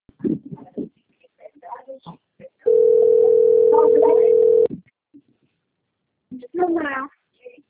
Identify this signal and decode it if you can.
Ringback tone (North America), 440 + 480 Hz, with speech.
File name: rg-1600-unknown- - - .35582